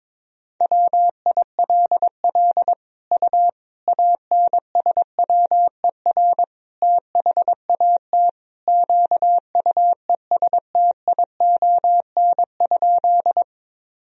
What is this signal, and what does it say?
Morse code, heard as a keyed tone, about 22 wpm, 700 Hz. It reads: WILL U ANSWER THAT QUESTION?
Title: Sentences from Top 400 Words and Abbreviations 22wpm